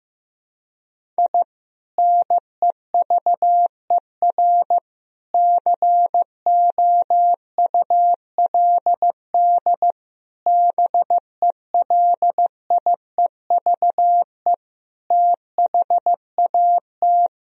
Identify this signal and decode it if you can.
Morse code, heard as a keyed tone, about 15 wpm, 700 Hz. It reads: I NEVER COULD BELIEVE THAT
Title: Sentences from Top 1000 Words - Review Part 1 - 15wpm